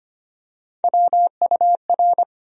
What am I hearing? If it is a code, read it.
Morse code, heard as a keyed tone, about 25 wpm, 700 Hz. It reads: WUR
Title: USA Call Sign Postfixes - 3 Letters 25wpm